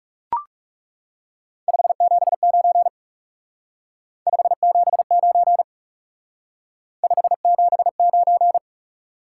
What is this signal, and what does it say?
Morse code, heard as a keyed tone, about 36 wpm, 700 Hz. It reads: 579 579 579
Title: Speed-Racing Top 180 QSO Elements 30wpm